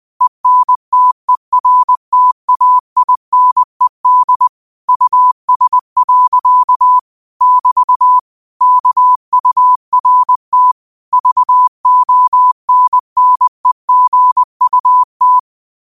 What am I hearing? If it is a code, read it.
Morse code, heard as a keyed tone, about 20 wpm, 1000 Hz. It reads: ENTERTAINED US. <BT> KURT VONNEGUT
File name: qotdfiles / 20wpm